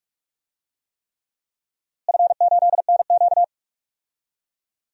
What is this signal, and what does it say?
Morse code, heard as a keyed tone, about 42 wpm, 700 Hz. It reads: F8NQ
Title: DXCC-100 Call Signs - 4 Characters - Repeated 3x - 45wpm